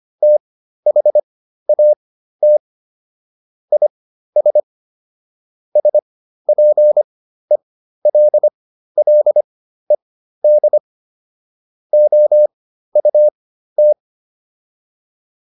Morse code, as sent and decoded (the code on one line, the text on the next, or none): - .... .- - / .. ... / ... .--. .-.. .-.. -.. / --- ..- -
THAT IS SPELLED OUT